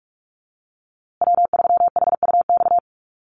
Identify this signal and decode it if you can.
Morse code, heard as a keyed tone, about 44 wpm, 700 Hz. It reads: W3HUX
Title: USA Call Signs Repeated 2x 45wpm